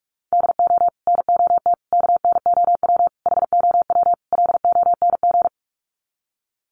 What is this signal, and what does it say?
Morse code, heard as a keyed tone, about 44 wpm, 700 Hz. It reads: DO NOT KNOW HOW LONG
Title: Sets of 5 Words Encore2 45wpm